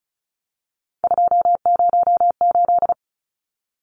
Morse code, heard as a keyed tone, about 34 wpm, 700 Hz. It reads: 208
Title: Numbers - 3-Digit 35wpm